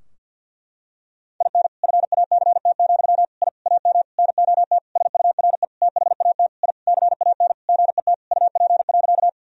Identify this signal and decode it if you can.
Morse code, heard as a keyed tone, about 47 wpm, 700 Hz. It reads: IN FACT, I AM NOT SURE THAT I CAN GET UP.
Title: Sentences from Top 1000 Words - Review Part 3 - 50wpm